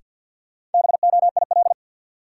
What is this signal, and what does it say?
Morse code, heard as a keyed tone, about 52 wpm, 700 Hz. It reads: BOIL
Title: Sets of 5 Words Encore - New Word Drill - 50wpm